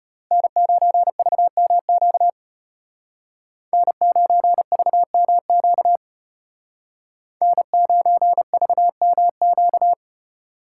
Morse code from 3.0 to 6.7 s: N9VMQ

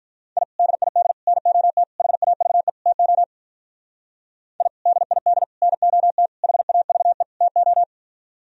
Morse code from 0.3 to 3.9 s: I DID NOT HAVE TO

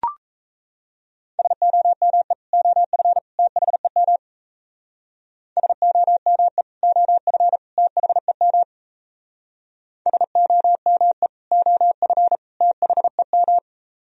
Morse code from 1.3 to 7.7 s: SOME OF THEM SOME OF

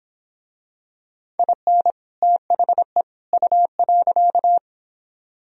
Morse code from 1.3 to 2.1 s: IN

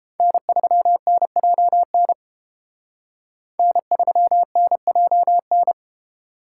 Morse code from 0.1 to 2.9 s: N3NJN

